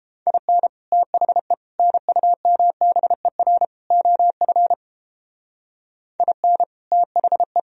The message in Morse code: IN THE NUMBER OF IN THE

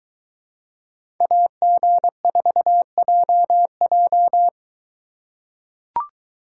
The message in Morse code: AG4JJ

